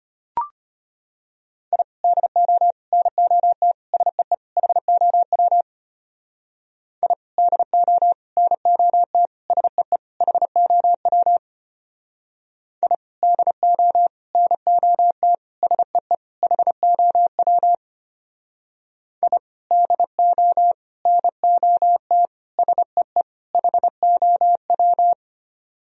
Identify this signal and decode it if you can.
Morse code, heard as a keyed tone, about 30 wpm, 700 Hz. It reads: I DO NOT SEE HOW I DO NOT SEE HOW I DO NOT SEE HOW I DO NOT SEE HOW
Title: Speed-Racing - Sets of 5 Words 25wpm